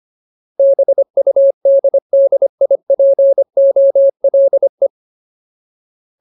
Morse code, heard as a keyed tone, about 25 wpm, 550 Hz. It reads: BUDDIPOLE
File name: Ultimate Word List / 25WPM Ultimate